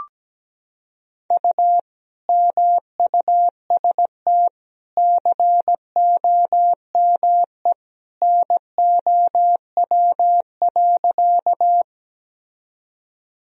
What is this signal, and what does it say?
Morse code, heard as a keyed tone, about 17 wpm, 700 Hz. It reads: U MUST COME NOW.
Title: Sentences from Top 200 Words and Abbreviations 17wpm